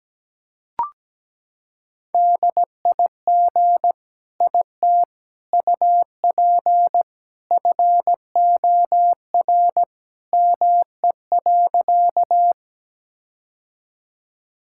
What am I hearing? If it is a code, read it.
Morse code, heard as a keyed tone, about 17 wpm, 700 Hz. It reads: DIG IT UP FOR ME.
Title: Sentences from Top 1600 Words 17wpm